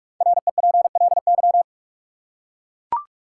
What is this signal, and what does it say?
Morse code, heard as a keyed tone, about 41 wpm, 700 Hz. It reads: REPLY
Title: Five Letter Words - 45wpm